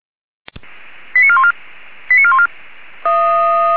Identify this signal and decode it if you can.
Five-tone selective call (ZVEI), heard as quick runs of five tones, tones of about 70 ms. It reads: 89414 89414